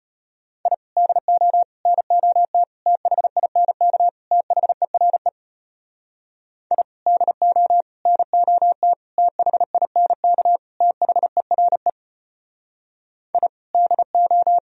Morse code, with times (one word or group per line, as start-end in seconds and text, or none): 0.7-0.7 I
1.0-1.6 DO
1.9-2.6 NOT
2.9-4.1 THINK
4.3-5.3 THERE
6.7-6.8 I
7.1-7.8 DO
8.1-8.9 NOT
9.2-10.6 THINK
10.8-11.9 THERE
13.3-13.5 I
13.7-14.6 DO